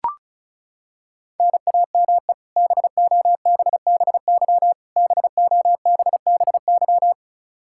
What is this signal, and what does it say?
Morse code, heard as a keyed tone, about 34 wpm, 700 Hz. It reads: NAME BOBBY BOBBY
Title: Practice QSO - Exchange 1 - Beginning - 35wpm